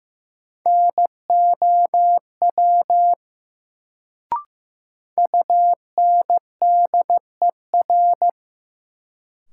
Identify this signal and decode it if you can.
Morse code, heard as a keyed tone, about 15 wpm, 700 Hz. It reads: NOW UNDER
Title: Sets of 2 Words - New Word Drill - 15wpm